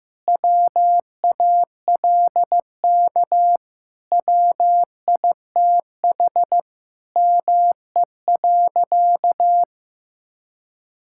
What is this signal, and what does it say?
Morse code, heard as a keyed tone, about 15 wpm, 700 Hz. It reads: WALK WITH ME.
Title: Sentences from Top 1000 Words - Review Part 6 - 15wpm